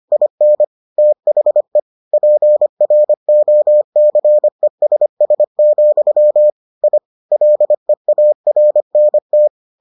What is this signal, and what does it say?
Morse code, heard as a keyed tone, about 25 wpm, 600 Hz. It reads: IN THE PROCESS, I LEARNT